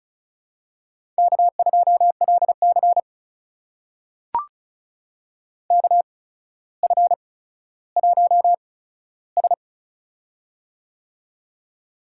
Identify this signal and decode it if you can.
Morse code, heard as a keyed tone, about 36 wpm, spaced as for 15 wpm, 700 Hz. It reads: K2LC KF1S
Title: USA Call Signs - 4 Characters - 8x Character Spacing - 35wpm